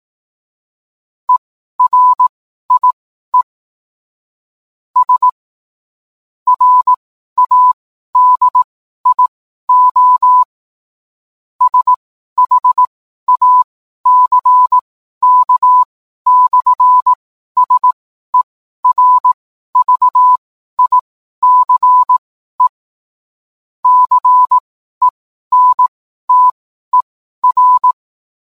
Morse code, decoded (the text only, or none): ERIE S RADIO SHACK/SERVICE CENTER